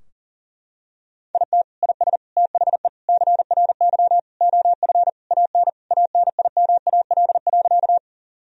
Morse code, heard as a keyed tone, about 40 wpm, 700 Hz. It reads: IT IS THE CRY OF AN ANIMAL.
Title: Sentences from Top 1000 Words - Review Part 2 - 40wpm